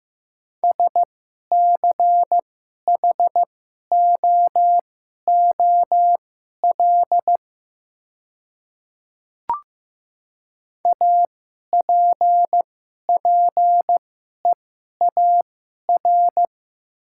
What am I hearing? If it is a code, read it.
Morse code, heard as a keyed tone, about 15 wpm, spaced as for 11 wpm, 700 Hz. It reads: SCHOOL APPEAR
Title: 2x Character Spacing - Top 500 Words - 15wpm